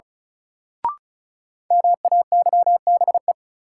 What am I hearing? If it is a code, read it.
Morse code, heard as a keyed tone, about 36 wpm, 700 Hz. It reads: MAYBE
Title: Sentences from Top 1000 Words - New Word Drill - 35wpm